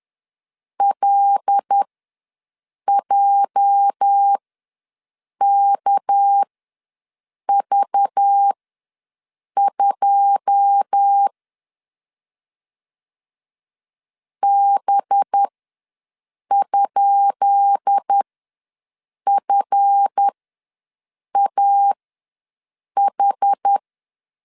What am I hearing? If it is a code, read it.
Morse code, heard as a keyed tone, about 10 wpm, spaced as for 6 wpm, 800 Hz. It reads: LJKV2 B?FAH
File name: LEZIONI / 10 SIGNS / 10B INT ws 10wpm 4134